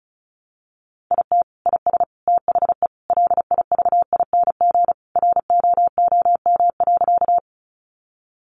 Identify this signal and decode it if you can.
Morse code, heard as a keyed tone, about 34 wpm, 700 Hz. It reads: IT IS THE LIVING ROOM.
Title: Sentences from Top 1100 Words 35wpm